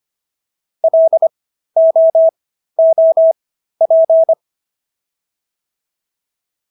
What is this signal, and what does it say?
Morse code, heard as a keyed tone, about 25 wpm, spaced as for 13 wpm, 650 Hz. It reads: LOOP